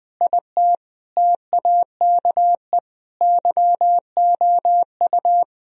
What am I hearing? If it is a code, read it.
Morse code, heard as a keyed tone, about 20 wpm, 700 Hz. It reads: IT TAKE YOU
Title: Sentences from Top 1000 Words - Review Part 2 - 20wpm